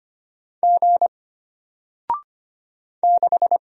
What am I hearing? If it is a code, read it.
Morse code, heard as a keyed tone, about 25 wpm, 700 Hz. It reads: G 6
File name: Single-Letter-Number-25wpm